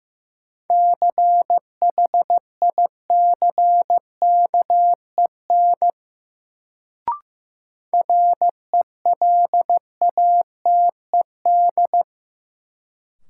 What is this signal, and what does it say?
Morse code, heard as a keyed tone, about 15 wpm, 700 Hz. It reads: CHICKEN RELATED
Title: Sentences from Top 1000 Words - Review Part 5 - New Word Drill - 15wpm